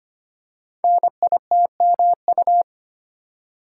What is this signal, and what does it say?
Morse code, heard as a keyed tone, about 25 wpm, 700 Hz. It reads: NITMU
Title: Code Groups - 5 Letter:Numbers - 25wpm